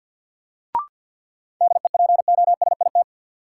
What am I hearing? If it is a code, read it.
Morse code, heard as a keyed tone, about 52 wpm, 700 Hz. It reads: DEPOSIT
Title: Sentences from Top 1300 Words - New Word Drill Alt - 50wpm